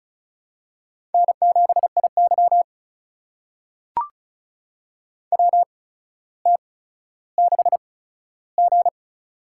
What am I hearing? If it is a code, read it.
Morse code, heard as a keyed tone, about 34 wpm, spaced as for 15 wpm, 700 Hz. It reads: N7IY WT6G
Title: USA Call Signs - 8x Character Spacing - 35wpm